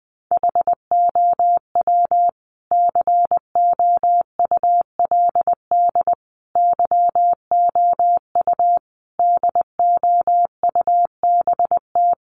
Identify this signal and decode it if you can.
Morse code, heard as a keyed tone, about 20 wpm, 700 Hz. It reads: HOW COULD YOU DOUBT
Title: Sentences from Top 1500 Words 20wpm